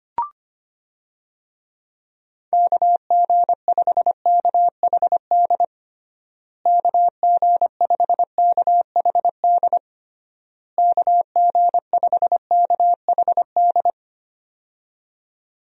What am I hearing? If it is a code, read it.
Morse code, heard as a keyed tone, about 25 wpm, 700 Hz. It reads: KG5KHD KG5KHD KG5KHD
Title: USA Call Signs - 6 Characters - Repeated 3x - 25wpm